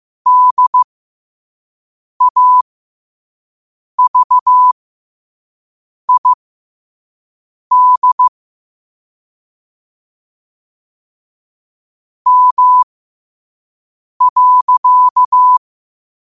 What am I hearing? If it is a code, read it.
Morse code, heard as a keyed tone, about 15 wpm, spaced as for 5 wpm, 1000 Hz. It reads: DAVID M.
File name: qotdfiles / 5wpm